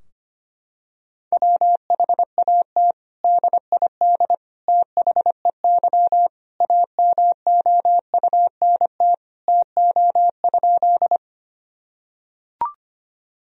Morse code, heard as a keyed tone, about 25 wpm, 700 Hz. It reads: WHAT DID THEY AMOUNT TO?
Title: Sentences from Top 1000 Words - Review Part 4 - 25wpm